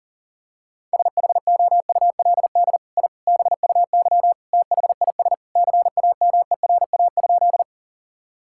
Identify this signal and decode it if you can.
Morse code, heard as a keyed tone, about 40 wpm, 690 Hz. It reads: SHOULD I BUY THIS CAMERA?
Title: Sentences from Top 1000 Words - Review Part 5 - 40wpm